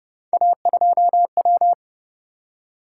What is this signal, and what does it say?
Morse code, heard as a keyed tone, about 29 wpm, 700 Hz. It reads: A2W